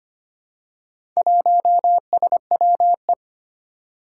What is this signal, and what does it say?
Morse code, heard as a keyed tone, about 25 wpm, 700 Hz. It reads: 1SWE
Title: Four Challenging Letter-Number Combinations 25wpm